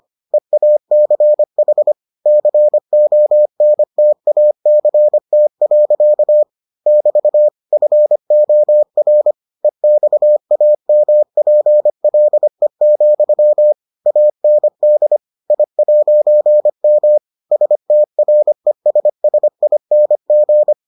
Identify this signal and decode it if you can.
Morse code, heard as a keyed tone, about 25 wpm, 600 Hz. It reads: EACH CONTACT. <BT> FOR EXAMPLE, AND I'M STRESSING